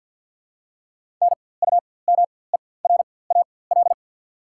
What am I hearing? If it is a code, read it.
Morse code, heard as a keyed tone, about 54 wpm, spaced as for 24 wpm, 700 Hz. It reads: NUMERAL
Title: 4x Character Spacing - Top 500 Words - 50wpm